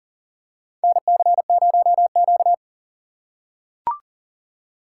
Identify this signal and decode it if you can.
Morse code, heard as a keyed tone, about 40 wpm, 700 Hz. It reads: NC0Q
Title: USA Call Signs - 3x Character Spacing - 40wpm